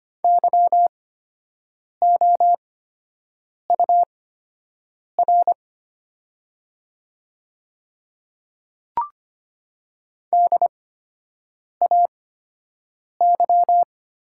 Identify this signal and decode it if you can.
Morse code, heard as a keyed tone, about 25 wpm, spaced as for 7 wpm, 700 Hz. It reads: YOUR DAY
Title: 8x Character Spacing - Top 100 Words - 25wpm